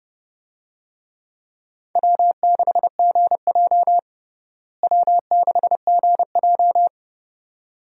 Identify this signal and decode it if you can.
Morse code, heard as a keyed tone, about 29 wpm, 700 Hz. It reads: W6GJ W6GJ
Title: USA Call Signs - 4 Characters - Repeated 2x - 30wpm